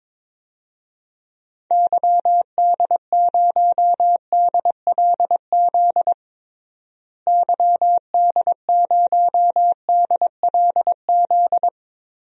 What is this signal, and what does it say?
Morse code, heard as a keyed tone, about 22 wpm, 700 Hz. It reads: YD0DLZ YD0DLZ
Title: DXCC-100 Call Signs - 6 Characters - Repeated 3x - 22wpm